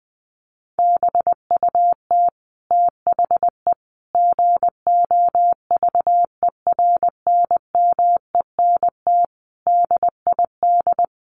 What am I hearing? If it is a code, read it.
Morse code, heard as a keyed tone, about 20 wpm, 700 Hz. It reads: BUT THE GOVERNMENT DID
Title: Sentences from Top 1400 Words 20wpm